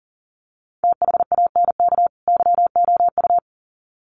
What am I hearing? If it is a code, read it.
Morse code, heard as a keyed tone, about 40 wpm, 700 Hz. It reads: THANK YOU
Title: Sets of 2 Words 40wpm